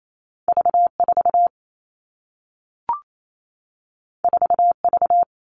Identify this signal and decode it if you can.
Morse code, heard as a keyed tone, about 28 wpm, 700 Hz. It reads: V 4 4 V
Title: Two Challenging Letter-Number Combinations 28wpm